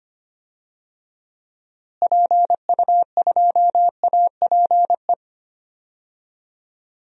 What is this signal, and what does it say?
Morse code, heard as a keyed tone, about 25 wpm, 700 Hz. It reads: PU2APE
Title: DXCC-100 Call Signs - 6 Characters - Repeated 3x - 25wpm